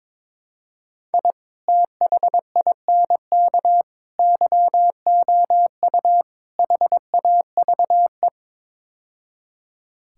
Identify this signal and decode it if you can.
Morse code, heard as a keyed tone, about 22 wpm, 700 Hz. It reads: I THINK YOU HAVE
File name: practice / 22wpm / Build A Phrase - 5-Word Phrases - 22wpm